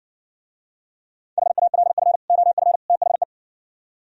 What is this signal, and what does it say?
Morse code, heard as a keyed tone, about 59 wpm, 700 Hz. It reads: HALF OF THE